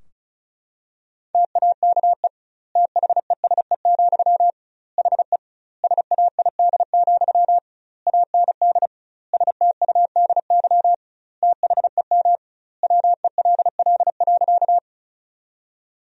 Morse code, heard as a keyed tone, about 36 wpm, 700 Hz. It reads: TAKE THESE, HE SAID, AND STUDY THEM WELL.
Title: 2x Word-Spacing - Sentences from Top 300 Words - 35wpm